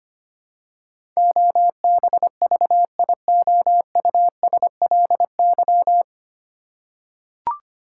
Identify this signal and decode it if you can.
Morse code, heard as a keyed tone, about 25 wpm, 700 Hz. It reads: OBVIOUSLY
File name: practice / 25wpm / 9 Letter Words - First 3 Letters Given - 25wpm